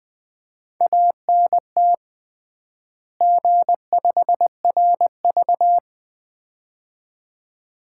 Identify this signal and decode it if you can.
Morse code, heard as a keyed tone, about 21 wpm, 700 Hz. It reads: ANT G5RV
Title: Common QSO Phrases - 3x Word-Spacing - 20wpm